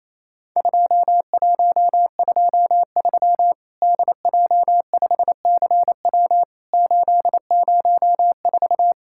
Morse code, heard as a keyed tone, about 28 wpm, 700 Hz. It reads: 2123 DJ5CW 804
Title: Worked All Europe DX Contest - QTC Exchange - 28wpm